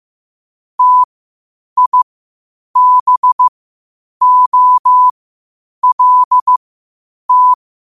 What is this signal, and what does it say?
Morse code, heard as a keyed tone, about 15 wpm, spaced as for 8 wpm, 1000 Hz. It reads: TIBOLT
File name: qotdfiles / 8wpm